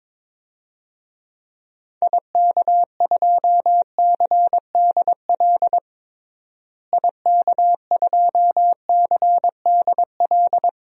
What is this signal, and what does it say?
Morse code, heard as a keyed tone, about 22 wpm, 700 Hz. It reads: IK2CDL IK2CDL
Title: DXCC-100 Call Signs - 6 Characters - Repeated 3x - 22wpm